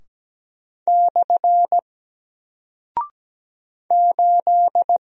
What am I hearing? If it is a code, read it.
Morse code, heard as a keyed tone, about 17 wpm, 700 Hz. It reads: / 8
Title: Single Character (Letter:Number:Punctuation:Prosign) 17wpm